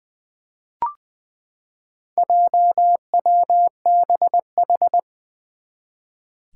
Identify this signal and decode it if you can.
Morse code, heard as a keyed tone, about 20 wpm, 700 Hz. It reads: JWBH
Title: Four Challenging Letter-Number Combinations 20wpm